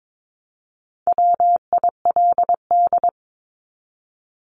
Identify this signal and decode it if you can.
Morse code, heard as a keyed tone, about 22 wpm, 700 Hz. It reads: WILD
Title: Top 1501-1750 Words 22wpm